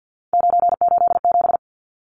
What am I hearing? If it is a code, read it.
Morse code, heard as a keyed tone, about 49 wpm, 700 Hz. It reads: 987
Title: Numbers - 3-Digit 50wpm